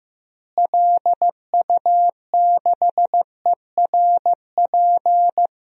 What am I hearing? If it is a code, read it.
Morse code, heard as a keyed tone, about 15 wpm, 700 Hz. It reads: LU6ERP
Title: DXCC-100 Call Signs - 6 Characters - Repeated 2x - 15wpm